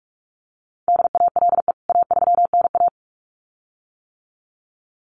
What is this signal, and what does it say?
Morse code, heard as a keyed tone, about 44 wpm, 700 Hz. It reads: DALE A3NA
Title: CWT Exchange 45wpm